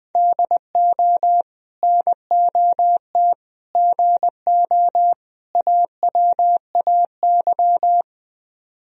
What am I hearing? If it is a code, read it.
Morse code, heard as a keyed tone, about 20 wpm, 700 Hz. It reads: DO NOT GO AWAY
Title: Speed-Racing - Sets of 4 Words 17wpm